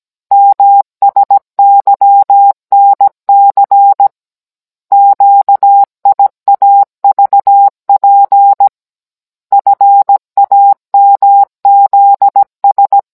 Morse code, heard as a keyed tone, about 17 wpm, 800 Hz. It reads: MSYNC QIAVP FAMZS
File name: Cours-NR19-16mm